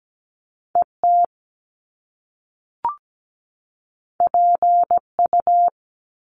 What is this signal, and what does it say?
Morse code, heard as a keyed tone, about 17 wpm, 700 Hz. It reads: ET PU
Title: Top Two Letter Combinations in Words 17wpm